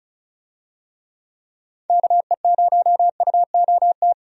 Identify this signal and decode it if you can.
Morse code, heard as a keyed tone, about 34 wpm, 700 Hz. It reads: KE0UOT
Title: USA Call Signs - 6 Characters - 35wpm